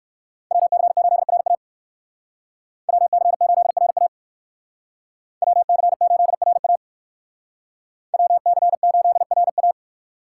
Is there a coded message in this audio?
Yes: Morse code from 0.5 to 9.7 s, heard as a keyed tone, about 62 wpm, 690 Hz. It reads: WC8RA WC8RA WC8RA WC8RA